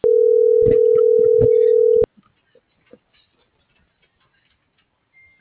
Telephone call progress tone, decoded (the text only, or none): ringback tone